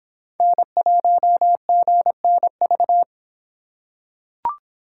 Morse code, heard as a keyed tone, about 26 wpm, 700 Hz. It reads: N1GNV